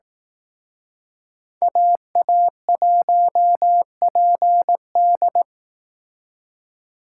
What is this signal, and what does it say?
Morse code, heard as a keyed tone, about 18 wpm, 700 Hz. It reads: AA1PD
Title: Speed Racing - USA 2x2 Call Signs 15wpm